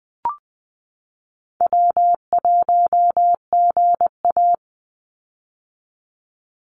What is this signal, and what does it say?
Morse code, heard as a keyed tone, about 20 wpm, 700 Hz. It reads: W1GA
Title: USA Call Signs 20wpm